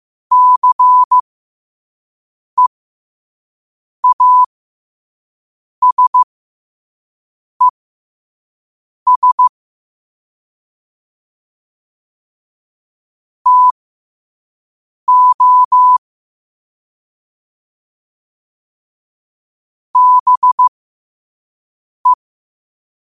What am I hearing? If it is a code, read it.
Morse code, heard as a keyed tone, about 15 wpm, spaced as for 5 wpm, 1000 Hz. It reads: CEASES TO BE
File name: qotdfiles / 5wpm